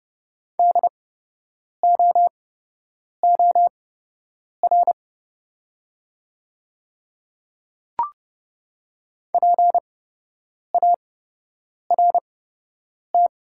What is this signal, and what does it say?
Morse code, heard as a keyed tone, about 30 wpm, spaced as for 8 wpm, 690 Hz. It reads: DOOR PART
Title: 8x Character Spacing - Top 500 Words - 30wpm